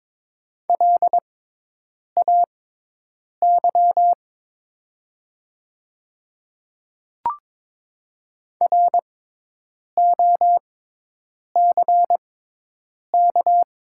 Morse code, heard as a keyed tone, about 22 wpm, spaced as for 8 wpm, 700 Hz. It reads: LAY ROCK
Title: 6x Character Spacing - Top 500 Words - 22wpm